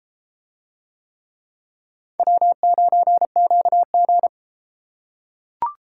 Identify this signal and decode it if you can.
Morse code, heard as a keyed tone, about 31 wpm, 700 Hz. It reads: W9QG